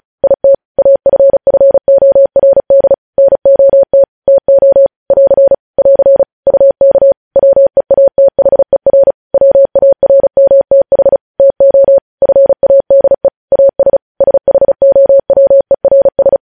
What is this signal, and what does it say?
Morse code, heard as a keyed tone, about 36 wpm, 550 Hz. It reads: IT AFFORD NOT TO <AR> <AR> UK WEATHER WARMTH TO FADE AS SHOWERS
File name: podcast / News headlines-1x-35x35